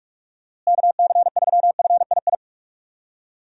Morse code, heard as a keyed tone, about 43 wpm, 700 Hz. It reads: KC3FII
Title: USA Call Signs - 6 Characters - Repeated 2x - 45wpm